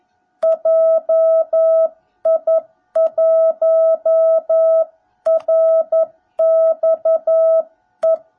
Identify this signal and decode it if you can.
Morse code, heard as a keyed tone, about 11 wpm, 660 Hz. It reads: JI1RXE
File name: jh8cbh / 2021cw / cw17b